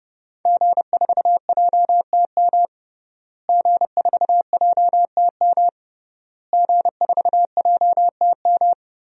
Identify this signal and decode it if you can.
Morse code, heard as a keyed tone, about 31 wpm, 700 Hz. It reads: G4JTM G4JTM G4JTM
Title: DXCC-100 Call Signs - 5 Characters - Repeated 3x - 30wpm